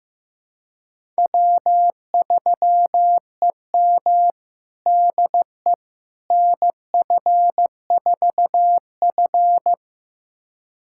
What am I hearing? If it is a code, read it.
Morse code, heard as a keyed tone, about 15 wpm, 700 Hz. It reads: W3EM DE NF4F
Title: USA Callsign DE Callsign - 4 Characters - 15wpm